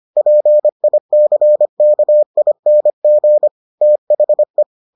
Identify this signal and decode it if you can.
Morse code, heard as a keyed tone, about 25 wpm, 600 Hz. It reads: PICKING THE